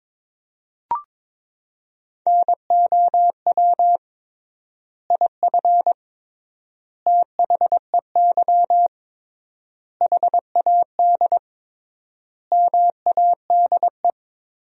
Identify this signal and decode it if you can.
Morse code, heard as a keyed tone, about 22 wpm, 700 Hz. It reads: NOW IF THEY HAD MADE
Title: 3x Word-Spacing - Sentences from Top 500 Words - 22wpm